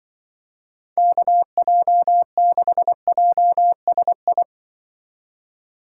Morse code, heard as a keyed tone, about 24 wpm, 700 Hz. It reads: KJ6JSI
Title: Speed Racing - USA 2x3 Call Signs 20wpm